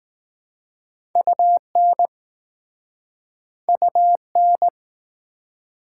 Morse code, heard as a keyed tone, about 19 wpm, 690 Hz. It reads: UN UN